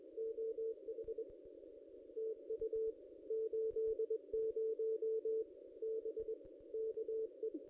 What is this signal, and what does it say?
Morse code, heard as a keyed tone, about 23 wpm, 440 Hz. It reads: OH TU 80 B KE
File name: ss-cw-k5tr-op-n5rz